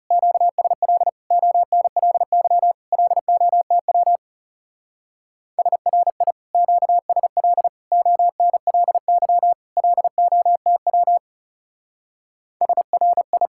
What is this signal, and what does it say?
Morse code, heard as a keyed tone, about 35 wpm, 700 Hz. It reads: QSL ONLY LOTW SRI QSL ONLY LOTW SRI
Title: Speed-Racing Common QSO Phrases - 30wpm